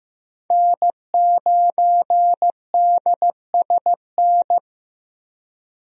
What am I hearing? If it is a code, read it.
Morse code, heard as a keyed tone, about 15 wpm, 700 Hz. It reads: N9DSN
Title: USA Call Signs - 5 Characters - 15wpm